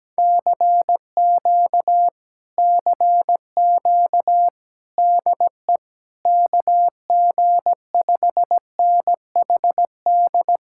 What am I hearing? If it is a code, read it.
Morse code, heard as a keyed tone, about 17 wpm, 700 Hz. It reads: CQ CQ DE KG5NHD
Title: Speed-Racing Common QSO Phrases - 17wpm